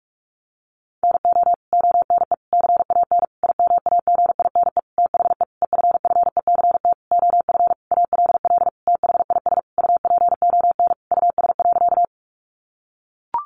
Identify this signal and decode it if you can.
Morse code, heard as a keyed tone, about 44 wpm, 700 Hz. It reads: NO ONE CAN IMAGINE THE EFFECT OF ALL THIS UPON US.
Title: Sentences from Top 1400 Words 45wpm